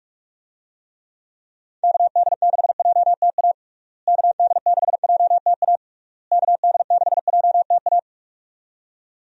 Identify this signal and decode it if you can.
Morse code, heard as a keyed tone, about 42 wpm, 700 Hz. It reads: KD6JTA KD6JTA KD6JTA
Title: USA Call Signs - 6 Characters - Repeated 3x - 45wpm